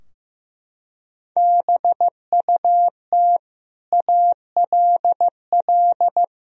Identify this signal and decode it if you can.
Morse code, heard as a keyed tone, about 16 wpm, 700 Hz. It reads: BUT ALL
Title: Sentences from Top 1000 Words - Review Part 3 - 15wpm